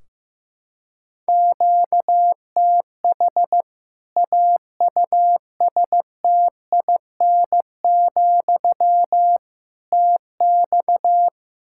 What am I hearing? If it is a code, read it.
Morse code, heard as a keyed tone, about 15 wpm, 700 Hz. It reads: QTH AUSTIN, TX
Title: Practice QSO - Exchange 1 - Beginning - 15wpm